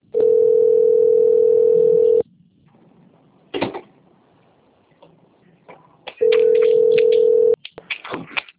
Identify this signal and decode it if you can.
Ringback tone (North America), 440 + 480 Hz.